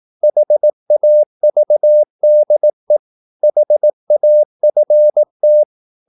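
Morse code, heard as a keyed tone, about 18 wpm, 600 Hz. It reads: HAVDE HAFT
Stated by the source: Slow (QRS) Morse practice at 18 wpm, CW med Gnister.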